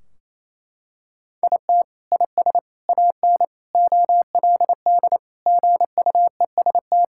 Morse code, heard as a keyed tone, about 28 wpm, 700 Hz. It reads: IT IS AN OLD GUEST
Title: Sentences from Top 1600 Words 28wpm